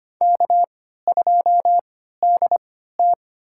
Morse code, heard as a keyed tone, about 25 wpm, spaced as for 14 wpm, 700 Hz. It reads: K2DT
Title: USA Call Signs - 4 Characters - 3x Character Spacing - 25wpm